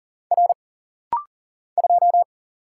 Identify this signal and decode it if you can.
Morse code, heard as a keyed tone, about 41 wpm, 700 Hz. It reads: R 2
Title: Single Letter-Number - Rapid-Fire 40wpm